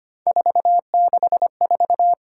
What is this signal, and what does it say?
Morse code, heard as a keyed tone, about 25 wpm, 700 Hz. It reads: 464
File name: Three Letter-Number Combinations - 25wpm